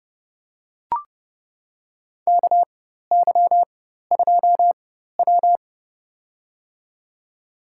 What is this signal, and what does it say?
Morse code, heard as a keyed tone, about 30 wpm, spaced as for 14 wpm, 700 Hz. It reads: KY2W
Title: USA Call Signs - 4x Character Spacing - 30wpm